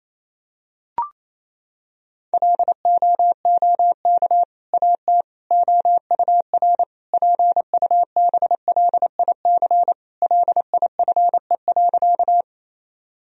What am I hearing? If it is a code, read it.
Morse code, heard as a keyed tone, about 27 wpm, 700 Hz. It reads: LOOK AT OUR PUBLIC LIFE.
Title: Sentences from Top 1000 Words - Review Part 3 - 28wpm